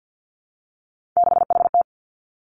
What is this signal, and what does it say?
Morse code, heard as a keyed tone, about 50 wpm, 700 Hz. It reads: 6HT
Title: Three Challenging Letter-Number Combinations 50wpm